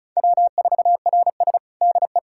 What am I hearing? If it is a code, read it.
Morse code, heard as a keyed tone, about 36 wpm, 700 Hz. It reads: W4RS DE
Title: USA Callsign DE Callsign - 4 Characters - 35wpm